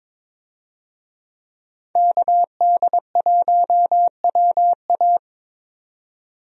Morse code, heard as a keyed tone, about 21 wpm, 700 Hz. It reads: KD1WA
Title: USA Call Signs - 5 Characters - 22wpm